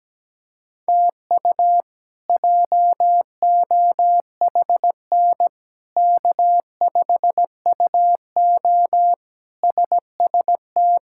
Morse code, heard as a keyed tone, about 17 wpm, 700 Hz. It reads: TU JOHN K5UO SST
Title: K1USN SST QSO Practice - 17wpm